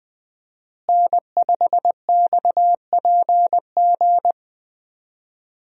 Morse code, heard as a keyed tone, about 20 wpm, 700 Hz. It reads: N5XPG
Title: USA Call Signs - 5 Characters - Repeated 3x - 20wpm